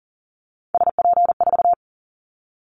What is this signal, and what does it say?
Morse code, heard as a keyed tone, about 40 wpm, 700 Hz. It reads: SP4
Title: Callsign Prefixes for Top 25 DXCC - Repeated 3x - 40wpm